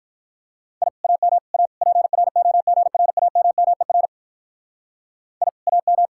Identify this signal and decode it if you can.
Morse code, heard as a keyed tone, about 45 wpm, 700 Hz. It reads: I AM A PROGRAMMER I AM